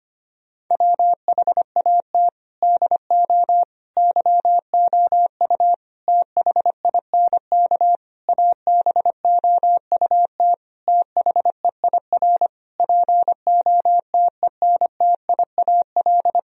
Morse code, heard as a keyed tone, about 25 wpm, 700 Hz. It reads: WHAT DO YOU THINK ABOUT THEIR POTENTIAL
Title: Sentences from Top 1000 Words - Review Part 4 - 25wpm